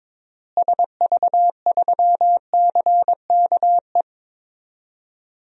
Morse code, heard as a keyed tone, about 22 wpm, 700 Hz. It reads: SV3CKE
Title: DXCC-100 Call Signs - 6 Characters - Repeated 3x - 22wpm